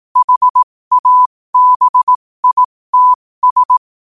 Morse code, heard as a keyed tone, about 18 wpm, spaced as for 15 wpm, 1000 Hz. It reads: HABITS
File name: qotdfiles / 15wpm